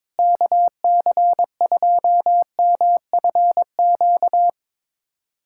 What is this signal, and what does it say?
Morse code, heard as a keyed tone, about 21 wpm, 700 Hz. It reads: KC2MFQ
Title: USA Callsign DE Callsign 22wpm